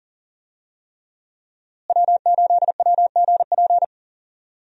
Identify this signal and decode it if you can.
Morse code, heard as a keyed tone, about 40 wpm, 700 Hz. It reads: W8WGP
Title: USA Call Signs - 5 Characters - Repeated 3x - 40wpm